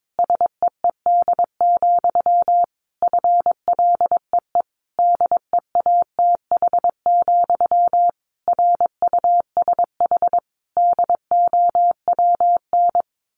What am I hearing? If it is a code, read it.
Morse code, heard as a keyed tone, about 22 wpm, 700 Hz. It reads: SEED, FLEE DEATH, RUSH DOWN
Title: A Sound Of Thunder 22wpm